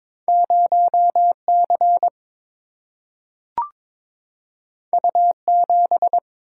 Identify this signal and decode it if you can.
Morse code, heard as a keyed tone, about 22 wpm, 700 Hz. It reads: 0 C U 7
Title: Single-Letter Single-Number 22wpm